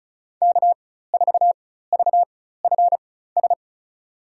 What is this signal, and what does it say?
Morse code, heard as a keyed tone, about 34 wpm, spaced as for 16 wpm, 700 Hz. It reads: K4VFS
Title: USA Call Signs - 5 Characters - 4x Character Spacing - 35wpm